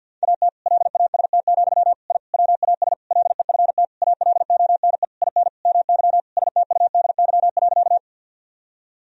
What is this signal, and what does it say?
Morse code, heard as a keyed tone, about 50 wpm, 700 Hz. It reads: AT LAST, I WAS LEFT ALONE IN MY STUDY.
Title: Sentences from Top 1000 Words - Review Part 1 - 50wpm